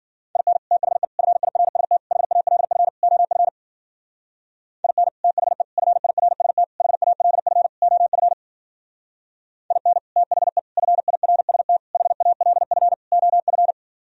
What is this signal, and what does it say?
Morse code, heard as a keyed tone, about 51 wpm, 700 Hz. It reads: IN THE FIRST HALF OF IN THE FIRST HALF OF IN THE FIRST HALF OF